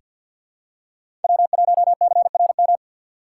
Morse code, heard as a keyed tone, about 49 wpm, 700 Hz. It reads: W1CRM
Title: USA Call Signs - 5 Characters - 2x Character Spacing - 50wpm